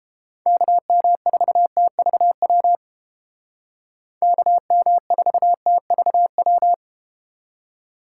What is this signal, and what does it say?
Morse code, heard as a keyed tone, about 31 wpm, 700 Hz. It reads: KM4TVW KM4TVW